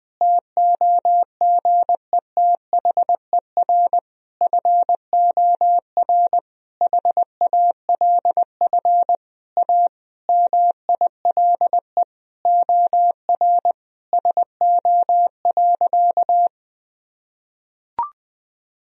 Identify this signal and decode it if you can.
Morse code, heard as a keyed tone, about 20 wpm, 700 Hz. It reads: TOGETHER FOR HALF A MILE OR SO.
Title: Sentences from Top 1000 Words - Review Part 1 - 20wpm